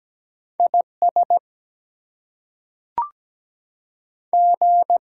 Morse code, heard as a keyed tone, about 17 wpm, 700 Hz. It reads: IS G
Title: Two Letter Words - 17wpm